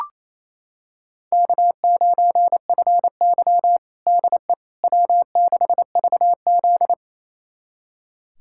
Morse code, heard as a keyed tone, about 28 wpm, 700 Hz. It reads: K9FY DE W6VZ
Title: USA Callsign DE Callsign - 4 Characters - 28wpm